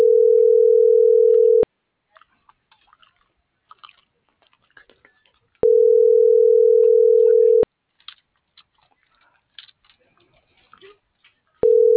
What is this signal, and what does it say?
Ringback tone (North America), 440 + 480 Hz.